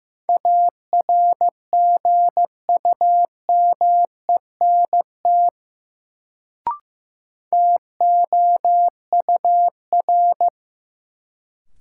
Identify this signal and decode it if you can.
Morse code, heard as a keyed tone, about 15 wpm, 700 Hz. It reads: ARGUMENT TOUR
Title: Sentences from Top 1400 Words - New Word Drill - 15wpm